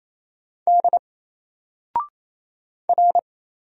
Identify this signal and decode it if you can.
Morse code, heard as a keyed tone, about 27 wpm, 700 Hz. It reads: D R
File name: Single-Letter-Number-28wpm